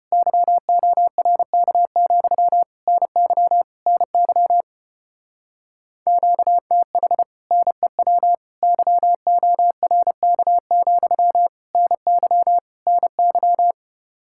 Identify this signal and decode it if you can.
Morse code, heard as a keyed tone, about 33 wpm, 700 Hz. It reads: YORK, NY NY QTH NEW YORK, NY NY